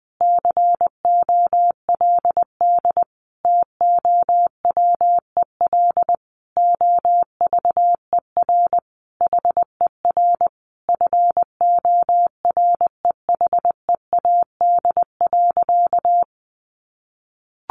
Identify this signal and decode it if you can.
Morse code, heard as a keyed tone, about 20 wpm, 700 Hz. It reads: COLD TOWEL OVER HER FOREHEAD.
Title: Easy Phrases 20wpm